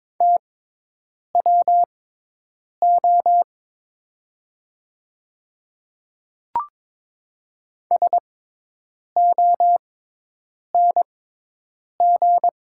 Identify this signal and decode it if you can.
Morse code, heard as a keyed tone, about 22 wpm, spaced as for 8 wpm, 700 Hz. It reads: TWO SONG
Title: 6x Character Spacing - Top 500 Words - 22wpm